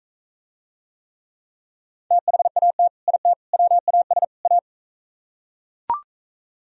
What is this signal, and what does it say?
Morse code, heard as a keyed tone, about 44 wpm, 700 Hz. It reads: THAT IT WAS A